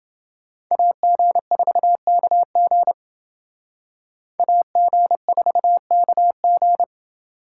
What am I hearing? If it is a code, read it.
Morse code, heard as a keyed tone, about 26 wpm, 700 Hz. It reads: AG4KG AG4KG